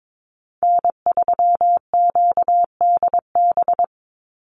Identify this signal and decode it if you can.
Morse code, heard as a keyed tone, about 22 wpm, 700 Hz. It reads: N3QDB
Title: USA Call Signs 22wpm